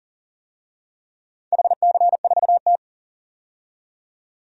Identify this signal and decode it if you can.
Morse code, heard as a keyed tone, about 37 wpm, 700 Hz. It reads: HC4T
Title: DXCC-100 Call Signs - 4 Characters - 40wpm